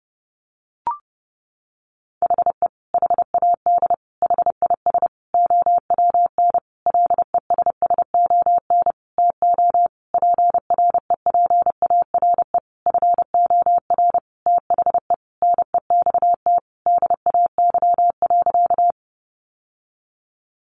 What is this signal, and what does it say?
Morse code, heard as a keyed tone, about 28 wpm, 700 Hz. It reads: HE HAD HIS OWN LESSON TO PREPARE FOR THE NEXT DAY.
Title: Sentences from Top 1200 Words 30wpm